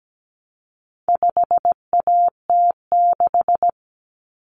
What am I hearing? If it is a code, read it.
Morse code, heard as a keyed tone, about 17 wpm, 700 Hz. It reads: 5AT6
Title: Common Cut Numbers - 4-Digit 17wpm